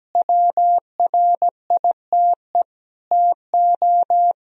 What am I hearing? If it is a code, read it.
Morse code, heard as a keyed tone, about 17 wpm, 700 Hz. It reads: WRITE TO
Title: Sentences from Top 1000 Words - Review Part 1 - 17wpm